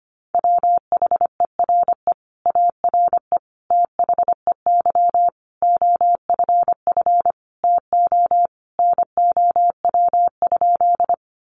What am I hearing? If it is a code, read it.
Morse code, heard as a keyed tone, about 25 wpm, 700 Hz. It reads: WHERE ARE THEY OFF TO NOW?
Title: Sentences from Top 200 Words 25wpm